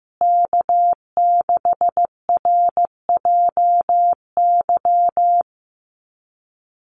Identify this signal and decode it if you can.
Morse code, heard as a keyed tone, about 15 wpm, 700 Hz. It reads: K6RJY
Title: USA Call Signs Repeated 2x 15wpm